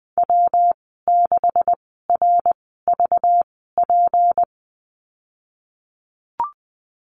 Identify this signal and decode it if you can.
Morse code, heard as a keyed tone, about 20 wpm, spaced as for 15 wpm, 700 Hz. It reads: W6RVP
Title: 193 Lesson 31g - Letter v - Call Signs 20wpm